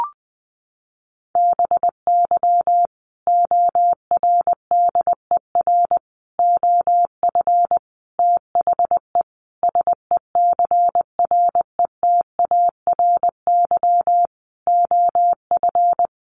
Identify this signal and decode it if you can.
Morse code, heard as a keyed tone, about 20 wpm, 700 Hz. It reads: BY ORDER OF THE SECRETARY OF
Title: Sentences from Top 1300 Words 20wpm